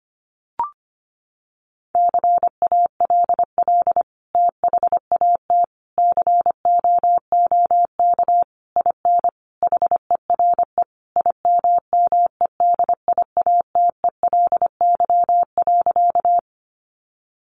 Morse code, heard as a keyed tone, about 25 wpm, 700 Hz. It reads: CALL THAT COOK IN HERE IMMEDIATELY.
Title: Sentences from Top 1100 Words 25wpm